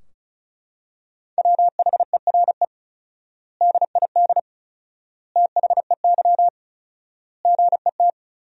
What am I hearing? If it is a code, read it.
Morse code, heard as a keyed tone, about 34 wpm, 700 Hz. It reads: WHERE DID THEY GET
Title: 4x Word-Spacing - Sentences from Top 400 Words - 35wpm